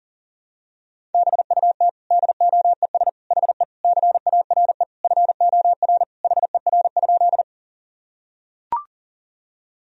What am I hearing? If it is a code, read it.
Morse code, heard as a keyed tone, about 40 wpm, 700 Hz. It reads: BUT DOES HE CARE FOR HER?
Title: Sentences from Top 1000 Words - Review Part 1 - 40wpm